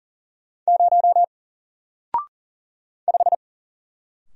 Morse code, heard as a keyed tone, about 40 wpm, 700 Hz. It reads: 0 5
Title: Single Letter-Number Alt 40wpm